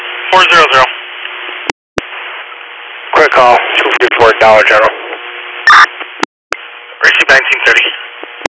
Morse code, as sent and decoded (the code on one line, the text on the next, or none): none